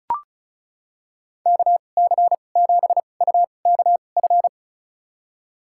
Morse code, heard as a keyed tone, about 36 wpm, spaced as for 25 wpm, 700 Hz. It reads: KC7UKF
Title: USA Call Signs - 2x Character Spacing - 35wpm